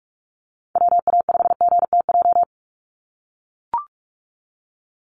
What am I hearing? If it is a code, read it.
Morse code, heard as a keyed tone, about 44 wpm, 700 Hz. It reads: WA5GTJ
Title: USA Call Signs 45wpm